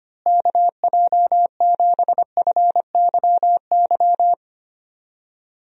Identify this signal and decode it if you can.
Morse code, heard as a keyed tone, about 25 wpm, 700 Hz. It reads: KJ7FYY
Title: USA Call Signs - 6 Characters - Repeated 3x - 25wpm